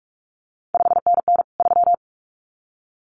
Morse code, heard as a keyed tone, about 46 wpm, 700 Hz. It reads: 5NN 3
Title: CQ WW DX Contest Exchange Encore - 45wpm